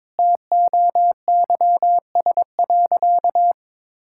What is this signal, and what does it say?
Morse code, heard as a keyed tone, about 22 wpm, 700 Hz. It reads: TOYS.
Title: Pangram Challenge v2 - 22wpm